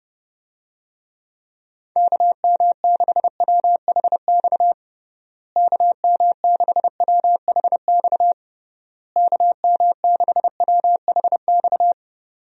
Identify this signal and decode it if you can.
Morse code, heard as a keyed tone, about 32 wpm, 700 Hz. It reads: KM6WHX KM6WHX KM6WHX
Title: USA Call Signs - 6 Characters - Repeated 3x - 30wpm